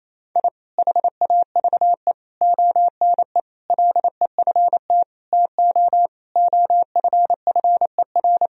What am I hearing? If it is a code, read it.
Morse code, heard as a keyed tone, about 28 wpm, 700 Hz. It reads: I HAVE ONE LEFT TO OFFER
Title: Sentences from Top 1000 Words - Review Part 4 - 28wpm